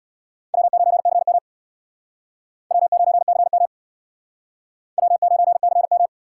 Morse code, heard as a keyed tone, about 66 wpm, 700 Hz. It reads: W9CM W9CM W9CM